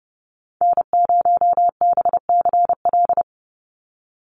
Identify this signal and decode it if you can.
Morse code, heard as a keyed tone, about 28 wpm, 700 Hz. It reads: N0BCL
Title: USA Call Signs 30wpm